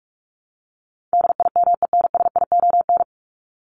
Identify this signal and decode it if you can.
Morse code, heard as a keyed tone, about 44 wpm, 700 Hz. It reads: DIMENSION
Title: Top 1501-1750 Words 45wpm